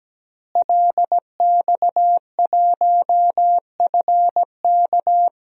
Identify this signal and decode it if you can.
Morse code, heard as a keyed tone, about 17 wpm, 700 Hz. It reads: LX1FK
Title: DXCC-100 Call Signs - 5 Characters - Repeated 2x - 17wpm